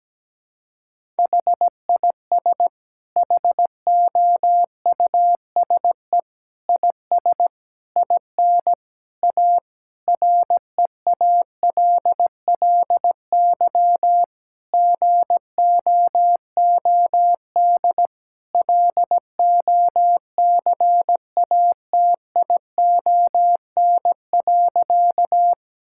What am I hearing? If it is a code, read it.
Morse code, heard as a keyed tone, about 17 wpm, 700 Hz. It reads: HIS HOUSE IS IN A REALLY GOOD LOCATION.
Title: Sentences from Top 1000 Words - Review Part 6 - 17wpm